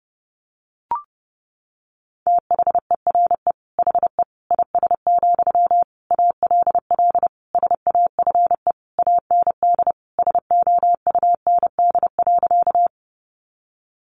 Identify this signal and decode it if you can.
Morse code, heard as a keyed tone, about 30 wpm, 700 Hz. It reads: THERE HE IS, ALL SAFE AND SOUND.
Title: Sentences from Top 900 Words 30wpm